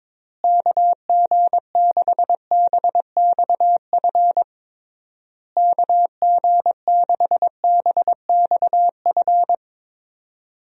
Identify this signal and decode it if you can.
Morse code, heard as a keyed tone, about 22 wpm, 700 Hz. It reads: KG6BXF KG6BXF
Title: USA Call Signs - 6 Characters - Repeated 3x - 22wpm